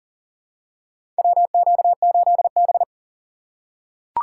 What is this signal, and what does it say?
Morse code, heard as a keyed tone, about 40 wpm, 700 Hz. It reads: WQ8B
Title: USA Call Signs - 2x Character Spacing - 40wpm